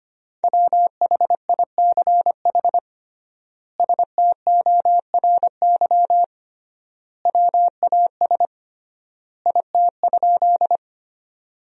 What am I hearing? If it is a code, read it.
Morse code, heard as a keyed tone, about 25 wpm, 700 Hz. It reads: WHICH STORY WAS IT?
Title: 3x Word-Spacing - Sentences from Top 300 Words - 25wpm